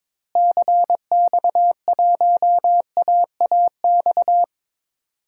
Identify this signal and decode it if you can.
Morse code, heard as a keyed tone, about 22 wpm, 700 Hz. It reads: CX1AAX
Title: DXCC-100 Call Signs - 6 Characters - Repeated 3x - 22wpm